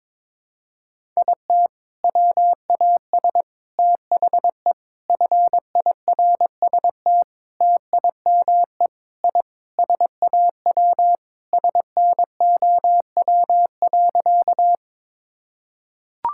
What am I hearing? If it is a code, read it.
Morse code, heard as a keyed tone, about 22 wpm, 700 Hz. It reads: IT WAS THE FIRST TIME I SAW SNOW.
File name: practice / 22wpm / Sentences with 4-Word Phrases - 22wpm